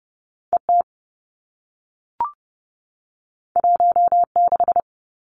Morse code, heard as a keyed tone, about 30 wpm, 700 Hz. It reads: E T 1 6
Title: Two Challenging Letter-Number Combinations 30wpm